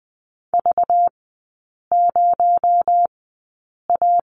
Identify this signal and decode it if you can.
Morse code, heard as a keyed tone, about 20 wpm, spaced as for 8 wpm, 700 Hz. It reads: V0A
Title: 218 Lesson 35e - Number 0 - Three Characters 20wpm